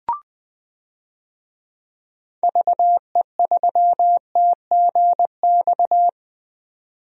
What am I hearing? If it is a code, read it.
Morse code, heard as a keyed tone, about 20 wpm, 700 Hz. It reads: VE3TGX
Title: DXCC-100 Call Signs - 6 Characters - 20wpm